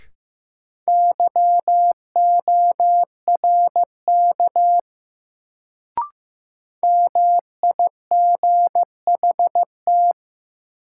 Morse code, heard as a keyed tone, about 15 wpm, 700 Hz. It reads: YORK MIGHT
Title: Sets of 2 Words - New Word Drill - 15wpm